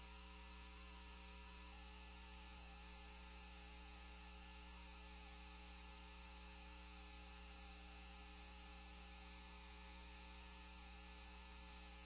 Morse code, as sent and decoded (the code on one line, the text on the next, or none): none